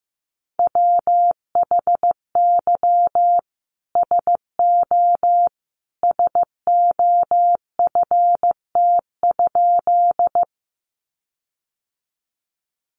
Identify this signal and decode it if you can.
Morse code, heard as a keyed tone, about 15 wpm, 700 Hz. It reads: WHY SO SOFT?
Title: Sentences from Top 1500 Words 15wpm